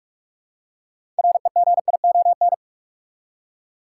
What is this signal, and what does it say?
Morse code, heard as a keyed tone, about 46 wpm, 700 Hz. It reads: REGION
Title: Six Letter Words - 45wpm